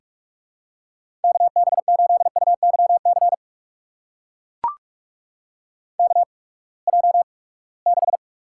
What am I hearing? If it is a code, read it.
Morse code, heard as a keyed tone, about 44 wpm, spaced as for 25 wpm, 700 Hz. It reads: KB8UYC KJ6
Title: USA Call Signs - 6 Characters - 8x Character Spacing - 45wpm